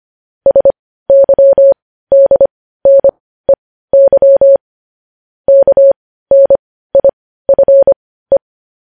Morse code, heard as a keyed tone, about 25 wpm, spaced as for 15 wpm, 550 Hz. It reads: SYDNEY KNIFE